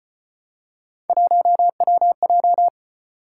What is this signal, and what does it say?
Morse code, heard as a keyed tone, about 34 wpm, 700 Hz. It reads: 1WJ